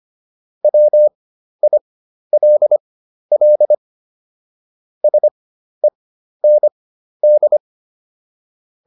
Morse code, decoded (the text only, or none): WILL SEND